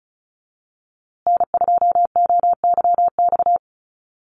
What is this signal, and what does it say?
Morse code, heard as a keyed tone, about 34 wpm, 700 Hz. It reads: N2OYX
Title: USA Call Signs Repeated 3x 35wpm